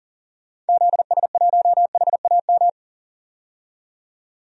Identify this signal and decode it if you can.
Morse code, heard as a keyed tone, about 39 wpm, 700 Hz. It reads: ZS1HAM
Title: DXCC-100 Call Signs - 6 Characters - 40wpm